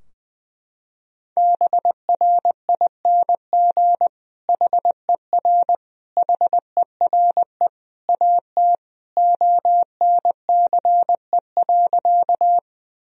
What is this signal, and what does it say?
Morse code, heard as a keyed tone, about 20 wpm, 700 Hz. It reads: BRING HER HERE AT ONCE.
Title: Sentences from Top 1000 Words - Review Part 3 - 20wpm